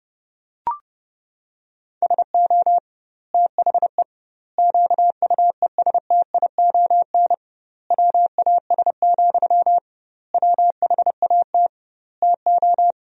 Morse code, heard as a keyed tone, about 30 wpm, 700 Hz. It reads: SO THE QUESTION WAS, WHAT TO
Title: 2x Word-Spacing - Sentences from Top 400 Words - 30wpm